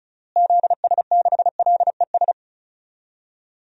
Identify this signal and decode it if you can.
Morse code, heard as a keyed tone, about 36 wpm, 700 Hz. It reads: ZS6LES
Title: DXCC-100 Call Signs - 6 Characters - Repeated 3x - 35wpm